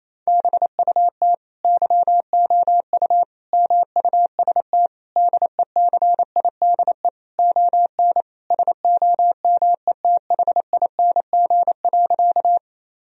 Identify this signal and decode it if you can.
Morse code, heard as a keyed tone, about 28 wpm, 700 Hz. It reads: BUT YOU MUST DECIDE ON SOMETHING.
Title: Sentences from Top 1000 Words - Review Part 3 - 28wpm